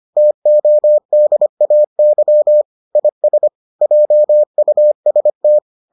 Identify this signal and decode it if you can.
Morse code, heard as a keyed tone, about 25 wpm, 600 Hz. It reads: TODAY IS JUST